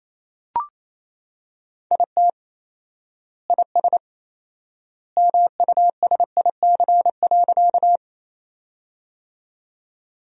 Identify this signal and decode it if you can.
Morse code, heard as a keyed tone, about 28 wpm, 700 Hz. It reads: IT IS MUSIC.
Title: 4x Word-Spacing - Sentences from Top 300 Words - 28wpm